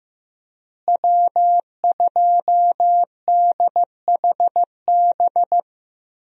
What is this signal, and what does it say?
Morse code, heard as a keyed tone, about 15 wpm, 700 Hz. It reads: W2DHB
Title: USA Call Signs - 5 Characters - Repeated 3x - 15wpm